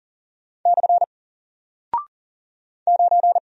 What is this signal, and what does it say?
Morse code, heard as a keyed tone, about 40 wpm, 700 Hz. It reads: / 9
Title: Single Character (Letter:Number:Punctuation:Prosign) Alt 40wpm